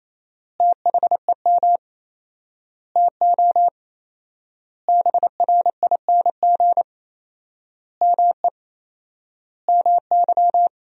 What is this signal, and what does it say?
Morse code, heard as a keyed tone, about 28 wpm, 700 Hz. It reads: THEM TO BRING ME MY
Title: 4x Word-Spacing - Sentences from Top 500 Words - 28wpm